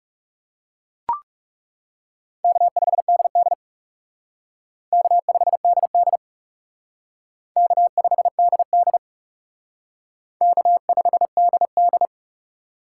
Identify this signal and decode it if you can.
Morse code, heard as a keyed tone, about 37 wpm, 700 Hz. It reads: K5DD K5DD K5DD K5DD
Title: Speed Racing - USA 1x2 Call Signs 30wpm